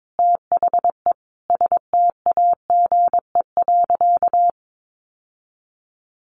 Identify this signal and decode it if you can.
Morse code, heard as a keyed tone, about 22 wpm, 700 Hz. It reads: THE STAGE.
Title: Sentences from Top 1200 Words 22wpm